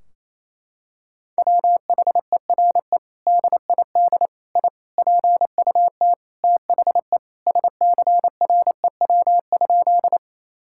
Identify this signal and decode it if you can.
Morse code, heard as a keyed tone, about 28 wpm, 700 Hz. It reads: WHERE DID I PUT THE SCREW?
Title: Sentences from Top 1600 Words 28wpm